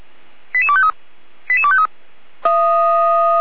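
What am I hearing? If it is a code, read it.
Five-tone selective call (ZVEI), heard as quick runs of five tones, tones of about 70 ms. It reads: 88253 88253
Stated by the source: Five-tone ZVEI selective call received on an SDR.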